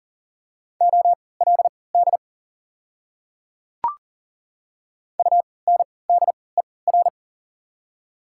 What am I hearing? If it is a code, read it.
Morse code, heard as a keyed tone, about 40 wpm, spaced as for 23 wpm, 700 Hz. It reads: OLD UNDER
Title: 3x Character Spacing - Top 300 Words - 40wpm